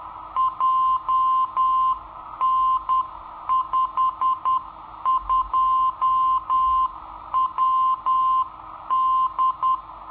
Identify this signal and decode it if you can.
Morse code, heard as a keyed tone, about 10 wpm, 1040 Hz. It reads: JN52WD
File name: sitoariold / beacon hk